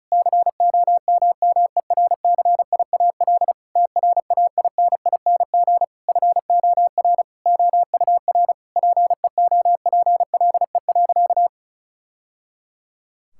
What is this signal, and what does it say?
Morse code, heard as a keyed tone, about 34 wpm, 700 Hz. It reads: COMMERCIAL TRAINING FOR OUR PEOPLE.
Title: Sentences from Top 1000 Words - Review Part 5 - 35wpm